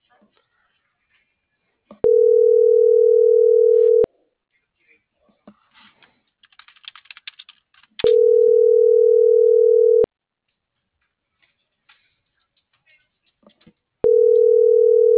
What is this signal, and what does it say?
Ringback tone (North America), 440 + 480 Hz.
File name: CENTRAL MABE / out antonio.garcia - .88998